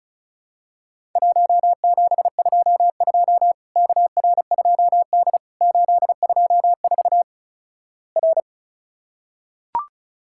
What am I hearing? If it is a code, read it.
Morse code, heard as a keyed tone, about 36 wpm, 700 Hz. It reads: 1722 KR2D 824 R
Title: Worked All Europe DX Contest - QTC Exchange - 35wpm